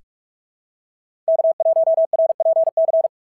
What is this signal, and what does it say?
Morse code, heard as a keyed tone, about 45 wpm, 690 Hz. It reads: K1RPC
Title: SOTA QSO Practice 45wpm